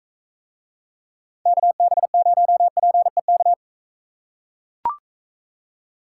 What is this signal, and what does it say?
Morse code, heard as a keyed tone, about 41 wpm, 700 Hz. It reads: KB0PEK